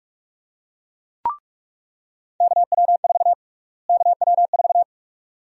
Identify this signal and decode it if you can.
Morse code, heard as a keyed tone, about 46 wpm, 700 Hz. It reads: KW4 KW4
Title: USA Call Sign Prefixes - Repeated 2x - 45wpm